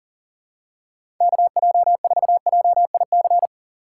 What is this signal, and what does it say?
Morse code, heard as a keyed tone, about 37 wpm, 700 Hz. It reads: KJ4JIC
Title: USA Call Signs - 6x Character Spacing - 40wpm